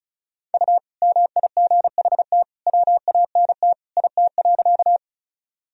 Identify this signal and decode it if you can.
Morse code, heard as a keyed tone, about 36 wpm, 700 Hz. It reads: U MIGHT WANT IT.
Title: Sentences from Top 300 Words and Abbreviations 35wpm